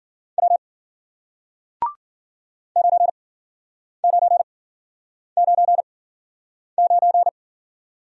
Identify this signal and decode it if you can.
Morse code, heard as a keyed tone, about 53 wpm, spaced as for 9 wpm, 700 Hz. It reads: W 9999